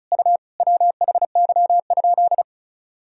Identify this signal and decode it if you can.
Morse code, heard as a keyed tone, about 34 wpm, 690 Hz. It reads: U WHY?
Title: Sentences from Top 200 Words and Abbreviations 35wpm